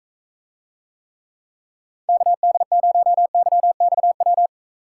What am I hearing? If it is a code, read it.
Morse code, heard as a keyed tone, about 44 wpm, 700 Hz. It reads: KD0YXW